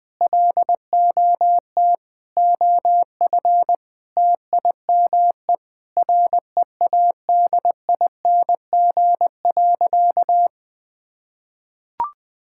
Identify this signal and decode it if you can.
Morse code, heard as a keyed tone, about 19 wpm, 700 Hz. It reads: LOT OF TIME READING.
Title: Sentences with 5-Word Phrases - Spoken Beginning - 20wpm